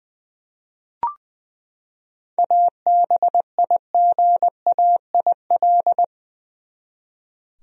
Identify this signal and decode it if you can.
Morse code, heard as a keyed tone, about 20 wpm, 700 Hz. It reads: ABIGAIL
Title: USA Common Names - 20wpm